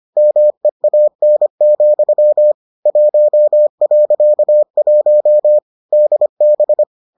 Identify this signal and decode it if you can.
Morse code, heard as a keyed tone, about 25 wpm, 600 Hz. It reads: MEAN, 1.1 DB